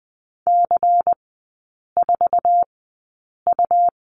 Morse code, heard as a keyed tone, about 20 wpm, spaced as for 8 wpm, 700 Hz. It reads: C4U
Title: 100 Lesson 17e - Letter u - Three Characters 20wpm